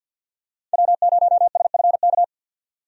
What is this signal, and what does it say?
Morse code, heard as a keyed tone, about 50 wpm, 700 Hz. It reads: W0SFK
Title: USA Call Signs - 5 Characters - 50wpm